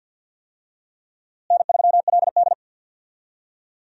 Morse code, heard as a keyed tone, about 46 wpm, 700 Hz. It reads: N3LD
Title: USA Call Signs - 4 Characters - Repeated 2x - 50wpm